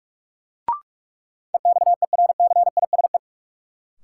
Morse code, heard as a keyed tone, about 46 wpm, 700 Hz. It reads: EXERCISE
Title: Sentences from Top 1200 Words - New Word Drill Alt - 45wpm